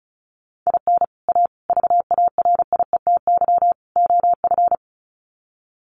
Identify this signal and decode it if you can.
Morse code, heard as a keyed tone, about 34 wpm, 700 Hz. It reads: IN A VARIETY OF
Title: Sets of 4 Words 35wpm